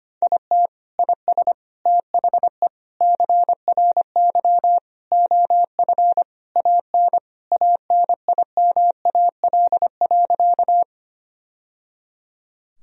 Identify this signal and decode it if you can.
Morse code, heard as a keyed tone, about 25 wpm, 700 Hz. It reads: IT IS THE CRY OF AN ANIMAL.
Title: Sentences from Top 1000 Words - Review Part 2 - 25wpm